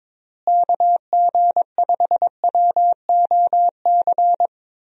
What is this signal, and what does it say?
Morse code, heard as a keyed tone, about 22 wpm, 700 Hz. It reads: KG5WOC
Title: USA Call Signs - 6 Characters - Repeated 3x - 22wpm